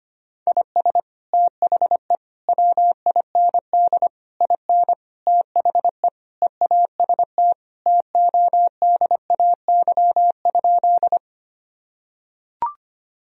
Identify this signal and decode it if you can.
Morse code, heard as a keyed tone, about 25 wpm, 700 Hz. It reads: IS THE WIND IN THE EAST TODAY?
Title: Sentences from Top 1000 Words - Review Part 3 - 25wpm